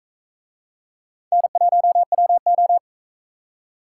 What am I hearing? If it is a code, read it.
Morse code, heard as a keyed tone, about 44 wpm, 700 Hz. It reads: N1WO